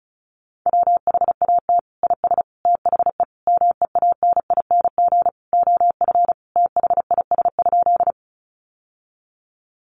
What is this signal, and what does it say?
Morse code, heard as a keyed tone, about 34 wpm, 700 Hz. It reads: WHAT IS THE MEANING OF THIS?
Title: Sentences from Top 900 Words 35wpm